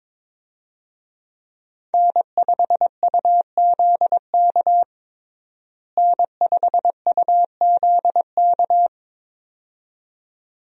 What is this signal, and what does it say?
Morse code, heard as a keyed tone, about 22 wpm, 700 Hz. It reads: N5UZK N5UZK
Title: USA Call Signs - 5 Characters - Repeated 2x - 22wpm